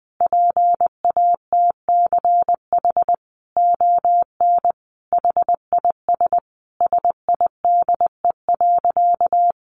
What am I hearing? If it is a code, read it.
Morse code, heard as a keyed tone, about 20 wpm, 700 Hz. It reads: PATCH ON HIS SIDE.
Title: A Sound Of Thunder 20wpm